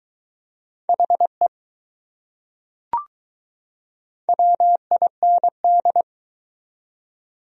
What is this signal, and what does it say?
Morse code, heard as a keyed tone, about 20 wpm, 700 Hz. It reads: HE WIND